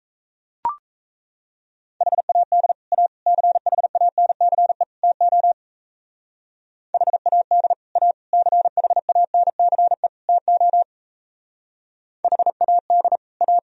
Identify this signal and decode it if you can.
Morse code, heard as a keyed tone, about 36 wpm, 700 Hz. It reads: HAD A CHANCE TO HAD A CHANCE TO HAD A